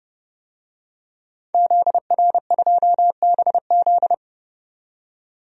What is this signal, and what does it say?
Morse code, heard as a keyed tone, about 31 wpm, 700 Hz. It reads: ZR2BZ
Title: DXCC-100 Call Signs - 5 Characters - 30wpm